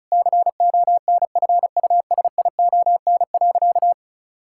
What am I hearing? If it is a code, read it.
Morse code, heard as a keyed tone, about 34 wpm, 700 Hz. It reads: CONFUSION.
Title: Sentences from Top 1600 Words 35wpm